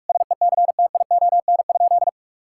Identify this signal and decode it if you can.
Morse code, heard as a keyed tone, about 46 wpm, 700 Hz. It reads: SECTION?
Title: Sentences from Top 1000 Words - Review Part 5 - 45wpm